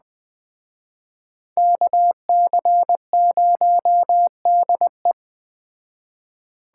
Morse code, heard as a keyed tone, about 21 wpm, 700 Hz. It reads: KC0DE
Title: Speed Racing - USA 2x2 Call Signs 17wpm